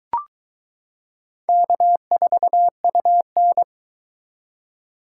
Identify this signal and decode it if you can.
Morse code, heard as a keyed tone, about 22 wpm, 700 Hz. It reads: K4UN